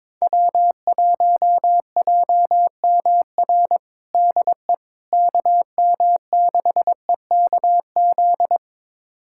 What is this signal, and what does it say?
Morse code, heard as a keyed tone, about 22 wpm, 700 Hz. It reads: W1JMR DE KM6EKZ
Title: USA Callsign DE Callsign 22wpm